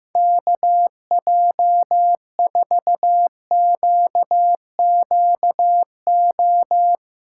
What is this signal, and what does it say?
Morse code, heard as a keyed tone, about 15 wpm, 700 Hz. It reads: KJ4QQO
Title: USA Call Signs - 6 Characters - 4x Character Spacing - 15wpm